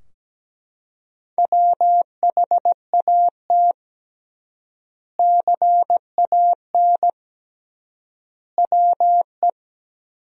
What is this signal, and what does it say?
Morse code, heard as a keyed tone, about 17 wpm, 700 Hz. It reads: WHAT CAN WE
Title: 3x Word-Spacing - Sentences from Top 100 Words - 17wpm